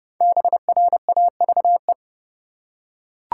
Morse code, heard as a keyed tone, about 30 wpm, 700 Hz. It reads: BRAVE
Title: Five Letter Words - 30wpm